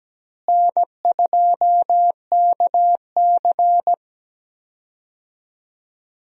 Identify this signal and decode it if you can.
Morse code, heard as a keyed tone, about 17 wpm, 700 Hz. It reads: N2KC
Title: USA Call Signs - 4 Characters - Repeated 3x - 17wpm